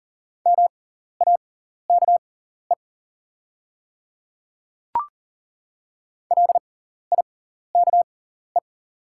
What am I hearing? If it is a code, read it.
Morse code, heard as a keyed tone, about 40 wpm, spaced as for 14 wpm, 700 Hz. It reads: MAKE LIKE
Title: 6x Character Spacing - Top 200 Words - 40wpm